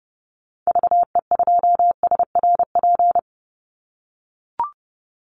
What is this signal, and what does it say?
Morse code, heard as a keyed tone, about 30 wpm, 700 Hz. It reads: VE2SRP
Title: Call Signs - 30wpm